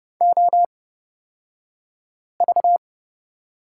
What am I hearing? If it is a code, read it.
Morse code, heard as a keyed tone, about 30 wpm, spaced as for 5 wpm, 700 Hz. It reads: OV
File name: Lesson 37d - Letter x - Two Characters 30wpm